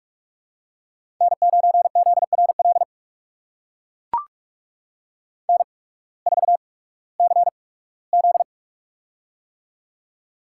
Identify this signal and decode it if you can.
Morse code, heard as a keyed tone, about 44 wpm, spaced as for 21 wpm, 700 Hz. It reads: N9ZRL N4CZ
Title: USA Call Signs - 8x Character Spacing - 45wpm